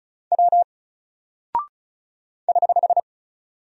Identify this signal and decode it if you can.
Morse code, heard as a keyed tone, about 36 wpm, 700 Hz. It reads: W <HH>
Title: Single Character (Letter:Number:Punctuation:Prosign) Alt 35wpm